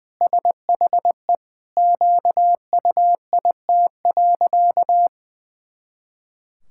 Morse code, heard as a keyed tone, about 19 wpm, 700 Hz. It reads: SHE QUIT.
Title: Pangram Challenge v2 - 20wpm